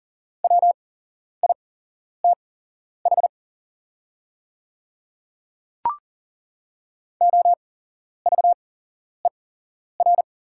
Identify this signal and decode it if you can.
Morse code, heard as a keyed tone, about 39 wpm, spaced as for 11 wpm, 700 Hz. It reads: WITH OVER
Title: 8x Character Spacing - Top 200 Words - 40wpm